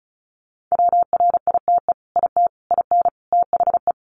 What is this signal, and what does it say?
Morse code, heard as a keyed tone, about 34 wpm, 700 Hz. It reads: WRITE IT IN THE
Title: Sentences from Top 600 Words 35wpm